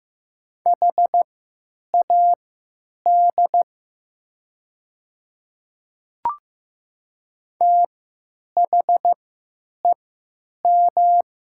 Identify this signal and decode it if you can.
Morse code, heard as a keyed tone, about 15 wpm, spaced as for 9 wpm, 690 Hz. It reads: HAD THEM
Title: 3x Character Spacing - Top 100 Words - 15wpm